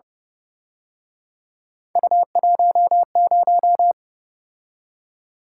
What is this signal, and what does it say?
Morse code, heard as a keyed tone, about 27 wpm, 700 Hz. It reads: U10
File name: Speed-Racing - Three Letter-Number Combinations - 22wpm